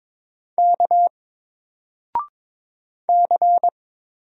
Morse code, heard as a keyed tone, about 22 wpm, 700 Hz. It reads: K C
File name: Single-Letter-Number-22wpm